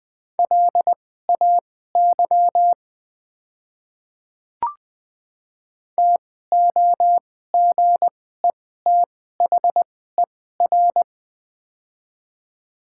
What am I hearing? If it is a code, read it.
Morse code, heard as a keyed tone, about 21 wpm, spaced as for 15 wpm, 700 Hz. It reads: LAY TOGETHER
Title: 2x Character Spacing - Top 500 Words - 20wpm